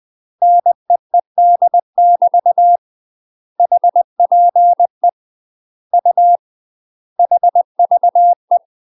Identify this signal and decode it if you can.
Morse code, heard as a keyed tone, about 20 wpm, 700 Hz. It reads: NEED <BT> HPE U HVE
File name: UK MASTER QSOs TRAINING / 20WPM / UK MASTER QSOs 0062